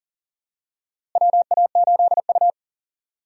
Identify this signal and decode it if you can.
Morse code, heard as a keyed tone, about 40 wpm, 700 Hz. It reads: WA8U
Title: USA Call Signs - 4 Characters - Repeated 2x - 40wpm